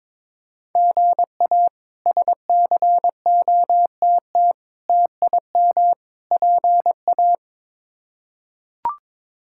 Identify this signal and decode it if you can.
Morse code, heard as a keyed tone, about 22 wpm, 700 Hz. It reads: GA SCOTT TIM PA
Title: K1USN SST QSO Practice - 22wpm